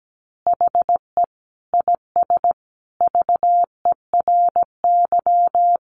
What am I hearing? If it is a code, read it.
Morse code, heard as a keyed tone, about 17 wpm, 700 Hz. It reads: HE IS VERY
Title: Easy Phrases 17wpm